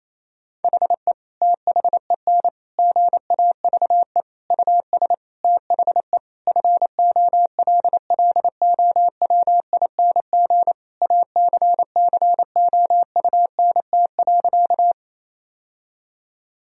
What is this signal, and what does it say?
Morse code, heard as a keyed tone, about 28 wpm, 700 Hz. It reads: HE THEN GAVE US THE FOLLOWING ACCOUNT.
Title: Sentences from Top 1000 Words - Review Part 4 - 28wpm